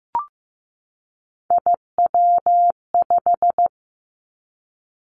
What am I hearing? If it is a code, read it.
Morse code, heard as a keyed tone, about 15 wpm, 700 Hz. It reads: IW5
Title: Three Challenging Letter-Number Combinations 15wpm